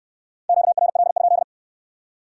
Morse code, heard as a keyed tone, about 67 wpm, 700 Hz. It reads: QRL?